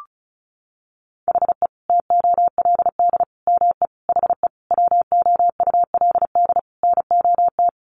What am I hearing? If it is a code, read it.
Morse code, heard as a keyed tone, about 34 wpm, 700 Hz. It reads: HE TOLD ME HE WOULD NOT
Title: Sentences from Top 1300 Words 35wpm